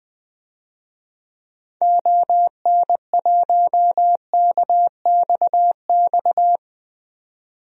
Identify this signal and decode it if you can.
Morse code, heard as a keyed tone, about 20 wpm, 700 Hz. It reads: ON1KXX
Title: DXCC-100 Call Signs - 6 Characters - 20wpm